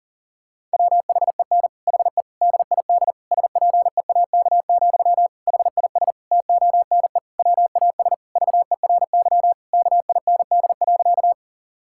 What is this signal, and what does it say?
Morse code, heard as a keyed tone, about 40 wpm, 700 Hz. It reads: WHEN HE DID SPEAK, HIS TONE WAS VERY KIND.
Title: Sentences from Top 1600 Words 40wpm